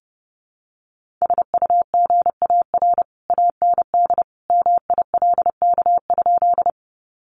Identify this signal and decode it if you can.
Morse code, heard as a keyed tone, about 30 wpm, 700 Hz. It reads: SUGAR AND MILK?
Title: Sentences from Top 1500 Words 30wpm